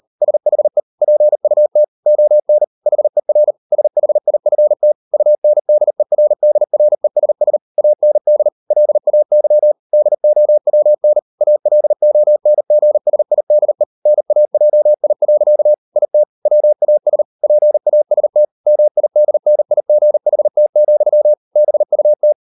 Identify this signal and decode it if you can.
Morse code, heard as a keyed tone, about 38 wpm, 600 Hz. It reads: SHE PUT ON HER SHIFT UNDERDRESS AND LAY DOWN ALONGSIDE NAJI. IT WAS PAST MIDNIGHT, BUT